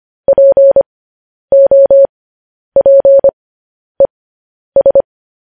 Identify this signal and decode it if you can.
Morse code, heard as a keyed tone, about 25 wpm, spaced as for 10 wpm, 550 Hz. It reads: POPES
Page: news headlines in morse code at 10 wpm